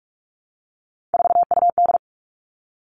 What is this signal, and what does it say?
Morse code, heard as a keyed tone, about 44 wpm, 700 Hz. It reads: 4UD
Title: All Cut Numbers - 3-Digit 45wpm